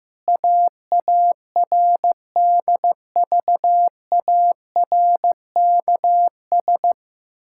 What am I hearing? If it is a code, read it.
Morse code, heard as a keyed tone, about 15 wpm, 700 Hz. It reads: AARDVARKS
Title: Pangram Challenge v2 - 15wpm